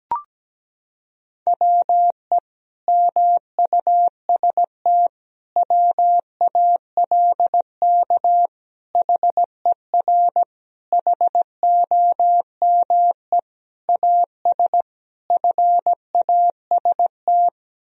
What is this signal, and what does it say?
Morse code, heard as a keyed tone, about 17 wpm, 700 Hz. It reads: WE MUST WALK HER HOME AS FAST
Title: Sentences from Top 1000 Words - Review Part 2 - 17wpm